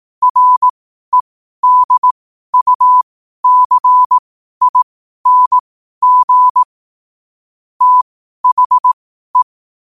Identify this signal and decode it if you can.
Morse code, heard as a keyed tone, about 18 wpm, spaced as for 12 wpm, 1000 Hz. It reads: REDUCING THE
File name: qotdfiles / 12wpm